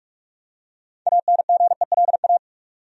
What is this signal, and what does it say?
Morse code, heard as a keyed tone, about 44 wpm, 700 Hz. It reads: ANGELA
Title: USA Common Names - 45wpm